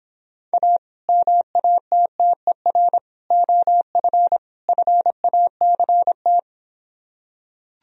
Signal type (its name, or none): Morse code